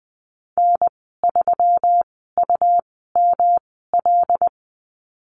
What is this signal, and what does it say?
Morse code, heard as a keyed tone, about 19 wpm, spaced as for 14 wpm, 700 Hz. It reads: N3UML
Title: 147 Lesson 24g - Letter y - Call Signs 20wpm